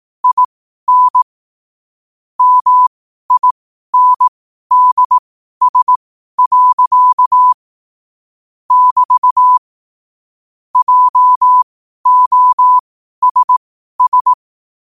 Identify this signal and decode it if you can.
Morse code, heard as a keyed tone, about 18 wpm, spaced as for 12 wpm, 1000 Hz. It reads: IN MINDS. <BT> JOSS
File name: 2026-01-07-12wpm